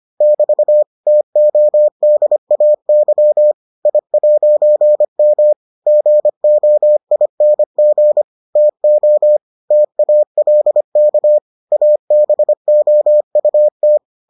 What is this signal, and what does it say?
Morse code, heard as a keyed tone, about 25 wpm, 600 Hz. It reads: <BT> TODAY I'M GOING TO TALK ABOUT